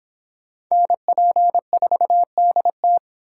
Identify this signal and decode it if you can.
Morse code, heard as a keyed tone, about 26 wpm, 700 Hz. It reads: NP4DT